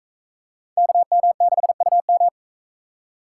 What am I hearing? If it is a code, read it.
Morse code, heard as a keyed tone, about 38 wpm, 700 Hz. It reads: KM6UM